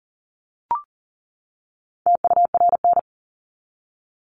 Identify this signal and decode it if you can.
Morse code, heard as a keyed tone, about 40 wpm, 700 Hz. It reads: TURN
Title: Top 200 Words 40wpm